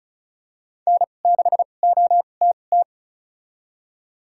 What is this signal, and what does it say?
Morse code, heard as a keyed tone, about 34 wpm, spaced as for 25 wpm, 700 Hz. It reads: N6OTT
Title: USA Call Signs - 5 Characters - 2x Character Spacing - 35wpm